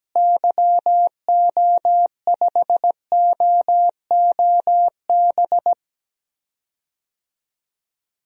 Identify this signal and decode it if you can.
Morse code, heard as a keyed tone, about 17 wpm, 700 Hz. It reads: YO5OOB
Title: DXCC-100 Call Signs - 6 Characters - Repeated 3x - 17wpm